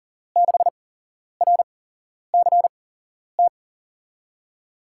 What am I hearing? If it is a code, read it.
Morse code, heard as a keyed tone, about 40 wpm, spaced as for 11 wpm, 700 Hz. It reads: 6RCT
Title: USA Call Signs - 6 Characters - 8x Character Spacing - 40wpm